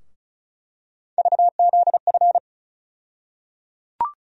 Morse code, heard as a keyed tone, about 36 wpm, 700 Hz. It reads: VZF
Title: USA Call Sign Postfixes - 3 Letters 35wpm